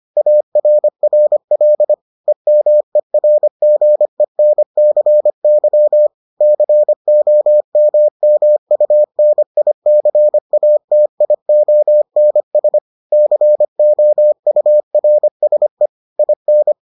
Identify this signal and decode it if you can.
Morse code, heard as a keyed tone, about 25 wpm, 600 Hz. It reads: ARRL EMERGENCY COMMUNICATIONS COURSE IN